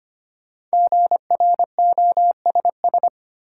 Morse code, heard as a keyed tone, about 25 wpm, 700 Hz. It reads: GROSS
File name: 5 Letter Words - First 3 Letters Given - 25wpm